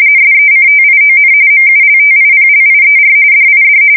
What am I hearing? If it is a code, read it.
RTTY, about 45.45 baud. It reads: SECTION MULTIPLIER DID
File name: fsk170 / teletype-RTTY